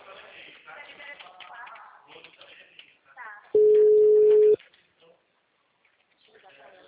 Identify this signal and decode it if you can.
Ringback tone (Europe, 425 Hz), with speech.